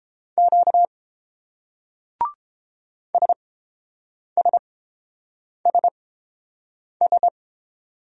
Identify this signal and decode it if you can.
Morse code, heard as a keyed tone, about 30 wpm, spaced as for 7 wpm, 700 Hz. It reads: Q SSSS